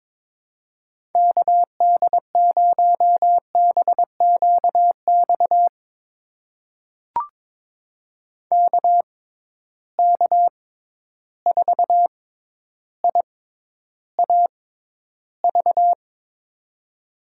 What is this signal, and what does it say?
Morse code, heard as a keyed tone, about 22 wpm, spaced as for 11 wpm, 700 Hz. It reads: KD0BQX KK4IAV
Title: USA Call Signs - 6 Characters - 6x Character Spacing - 22wpm